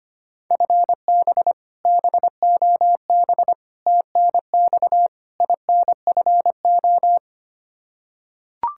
Morse code, heard as a keyed tone, about 25 wpm, 700 Hz. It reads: FB BOB TNX INFO
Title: Practice QSO - Exchange 3 - More Info - 25wpm